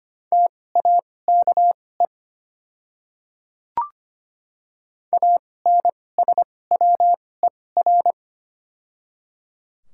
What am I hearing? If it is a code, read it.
Morse code, heard as a keyed tone, about 25 wpm, spaced as for 18 wpm, 700 Hz. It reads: TAKE ANSWER
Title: 2x Character Spacing - Top 500 Words - 25wpm